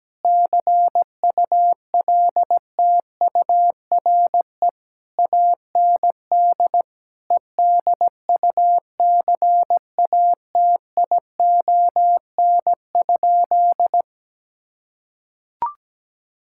Morse code, heard as a keyed tone, about 17 wpm, 700 Hz. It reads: CULTURE AND EDUCATION?
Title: Sentences from Top 1000 Words - Review Part 6 - 17wpm